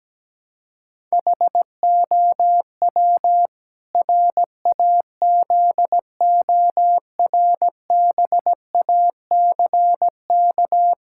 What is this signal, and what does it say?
Morse code, heard as a keyed tone, about 17 wpm, 700 Hz. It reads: HOW RAZORBACK
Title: Pangram Challenge v2 - 17wpm